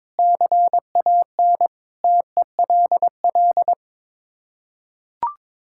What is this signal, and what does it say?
Morse code, heard as a keyed tone, about 22 wpm, 700 Hz. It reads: CAN TELL
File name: practice / 22wpm / Sets of 6 Words - 22wpm